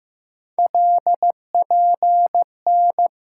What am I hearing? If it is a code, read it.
Morse code, heard as a keyed tone, about 15 wpm, 700 Hz. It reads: LPN
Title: Practice QSO - Exchange 3 - More Info - 15wpm